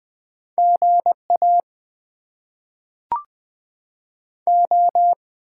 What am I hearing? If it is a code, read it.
Morse code, heard as a keyed tone, about 20 wpm, 700 Hz. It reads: GA O
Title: Top 90 QSO Elements - 8x Character Spacing - 20wpm